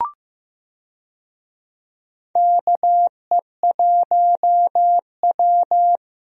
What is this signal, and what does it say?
Morse code, heard as a keyed tone, about 15 wpm, 700 Hz. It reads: KE1W
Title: USA Call Signs - 4 Characters - Repeated 3x - 15wpm